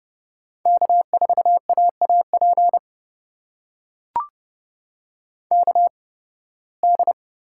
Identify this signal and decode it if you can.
Morse code, heard as a keyed tone, about 31 wpm, spaced as for 20 wpm, 700 Hz. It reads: K4AAP KD
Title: USA Call Signs - 5 Characters - 8x Character Spacing - 30wpm